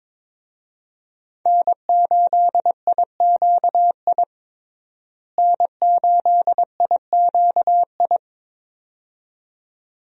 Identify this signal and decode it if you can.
Morse code, heard as a keyed tone, about 23 wpm, 700 Hz. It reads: N8IQI N8IQI
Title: USA Call Signs - 5 Characters - Repeated 2x - 22wpm